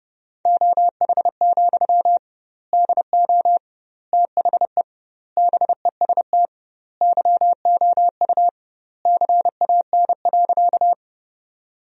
Morse code, heard as a keyed tone, about 32 wpm, 700 Hz. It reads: OH, DO THE BEST YOU CAN.
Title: 2x Word-Spacing - Sentences from Top 500 Words - 30wpm